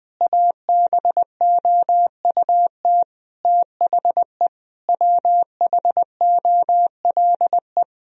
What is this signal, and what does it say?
Morse code, heard as a keyed tone, about 20 wpm, 700 Hz. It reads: ABOUT THE WHOLE
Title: Sentences from Top 1000 Words - Review Part 3 - 20wpm